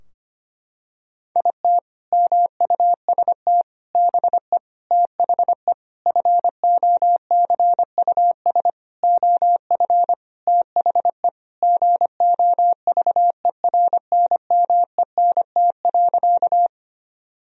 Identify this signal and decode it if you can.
Morse code, heard as a keyed tone, about 25 wpm, 700 Hz. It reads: IT MUST BE THE FOCUS OF THE GOVERNMENT.
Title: Sentences from Top 1000 Words - Review Part 4 - 25wpm